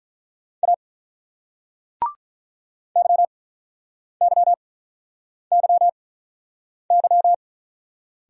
Morse code, heard as a keyed tone, about 46 wpm, spaced as for 9 wpm, 700 Hz. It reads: A YYYY